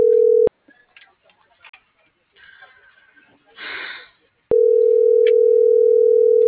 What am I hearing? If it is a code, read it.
Ringback tone (North America), 440 + 480 Hz.